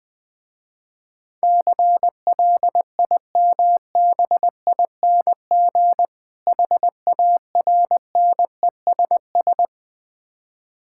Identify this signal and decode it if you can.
Morse code, heard as a keyed tone, about 21 wpm, 700 Hz. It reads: CLIMBING HARNESS
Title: Difficult Words with Hints - 20wpm